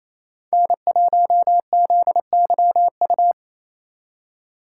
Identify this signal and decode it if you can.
Morse code, heard as a keyed tone, about 35 wpm, 700 Hz. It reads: N1ZYU
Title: Speed Racing - USA 1x3 Call Signs 28wpm